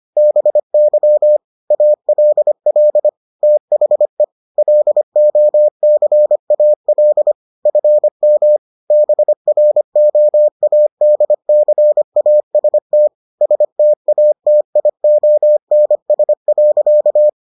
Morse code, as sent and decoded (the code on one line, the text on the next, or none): -... -.-- / .- .-.. .-.. / - .... / .-.. --- -.-. .- .-.. / ..-. -- / -... .-. --- .- -.. -.-. .- ... - / ... - .- - .. --- -. ... .-.-.-
BY ALL THE LOCAL FM BROADCAST STATIONS.